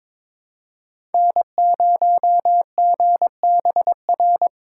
Morse code, heard as a keyed tone, about 22 wpm, 700 Hz. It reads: N0GBR
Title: USA Call Signs - 5 Characters - 3x Character Spacing - 22wpm